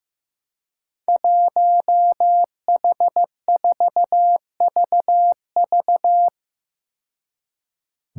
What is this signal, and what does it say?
Morse code, heard as a keyed tone, about 15 wpm, 700 Hz. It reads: 1H4VV
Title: Five Challenging Letter-Number Combinations 15wpm